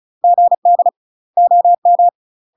Morse code, heard as a keyed tone, about 36 wpm, 700 Hz. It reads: GD OM
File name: UK MASTER QSOs TRAINING / 35WPM / UK MASTER QSOs 0022